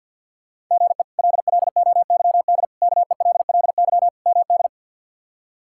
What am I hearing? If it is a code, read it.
Morse code, heard as a keyed tone, about 46 wpm, 690 Hz. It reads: GE LLOYD KELLY MD
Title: K1USN SST QSO Practice - 50wpm